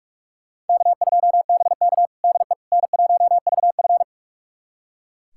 Morse code, heard as a keyed tone, about 42 wpm, 700 Hz. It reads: K2BK DE N1VF
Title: USA Callsign DE Callsign - 4 Characters - 45wpm